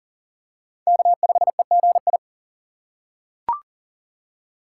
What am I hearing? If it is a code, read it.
Morse code, heard as a keyed tone, about 40 wpm, 700 Hz. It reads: K5EGI
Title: USA Call Signs - 5 Characters - 4x Character Spacing - 40wpm